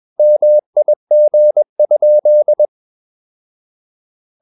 Morse code, heard as a keyed tone, about 21 wpm, 600 Hz. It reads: MIG?
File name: Ferie QRS 21wpm